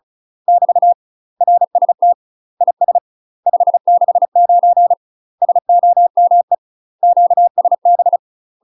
Morse code, heard as a keyed tone, about 36 wpm, 700 Hz. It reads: <BT> RST IS 569 SOME QSB
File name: UK MASTER QSOs TRAINING / 35WPM / UK MASTER QSOs 0002